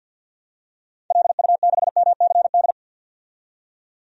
Morse code, heard as a keyed tone, about 52 wpm, 700 Hz. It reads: LU6MCD
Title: DXCC-100 Call Signs - 6 Characters - 50wpm